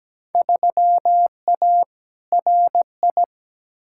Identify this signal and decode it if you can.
Morse code, heard as a keyed tone, about 17 wpm, 700 Hz. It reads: 3A RI
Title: Field Day Exchanges - 17wpm